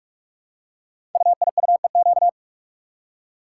Morse code, heard as a keyed tone, about 44 wpm, 700 Hz. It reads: UIUEQ
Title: Code Groups - 5 Letters - 45wpm